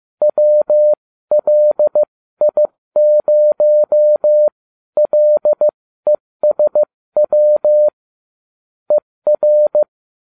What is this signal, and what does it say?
Morse code, heard as a keyed tone, about 15 wpm, spaced as for 13 wpm, 600 Hz. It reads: WLI0LESW ER